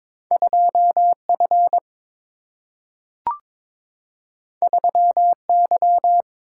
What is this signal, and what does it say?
Morse code, heard as a keyed tone, about 22 wpm, 700 Hz. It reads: 2F 3Y
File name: practice / 22wpm / Two Letter-Number Combinations - 22wpm